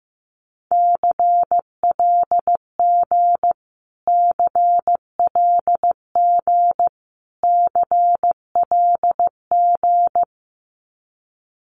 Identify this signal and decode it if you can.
Morse code, heard as a keyed tone, about 14 wpm, 700 Hz. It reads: CLG CLG CLG
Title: Abbreviations Repeated 3x - 15wpm